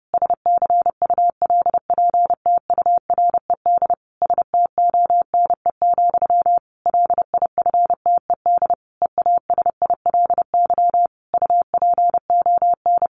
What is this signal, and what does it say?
Morse code, heard as a keyed tone, about 30 wpm, 700 Hz. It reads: SCULPTURED STONE, LIFTED EASILY UPON
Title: A Sound Of Thunder 30wpm